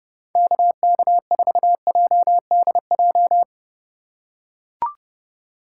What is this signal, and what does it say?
Morse code, heard as a keyed tone, about 31 wpm, 700 Hz. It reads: KK4JDJ
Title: USA Call Signs - 4x Character Spacing - 30wpm